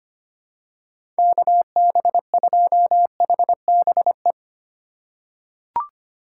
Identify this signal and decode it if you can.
Morse code, heard as a keyed tone, about 25 wpm, 700 Hz. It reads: KB2HBE
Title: USA Call Signs - 6 Characters - 4x Character Spacing - 25wpm